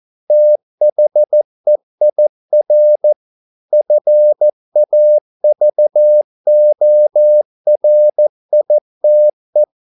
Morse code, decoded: THEIR FAVORITE

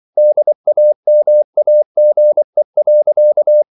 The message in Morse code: DAMAGE.

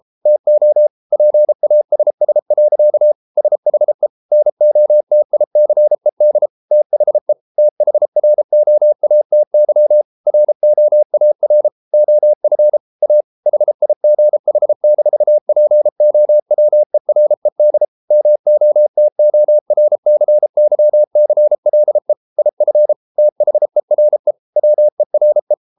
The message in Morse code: TO PASS. SHE NOTICED THE THROATY ROAR OF A HIGH-POWERED MOTORCYCLE IF THERE WERE

